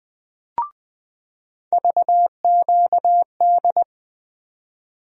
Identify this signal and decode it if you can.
Morse code, heard as a keyed tone, about 20 wpm, 700 Hz. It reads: VQD
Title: USA Call Sign Postfixes - 3 Letters 20wpm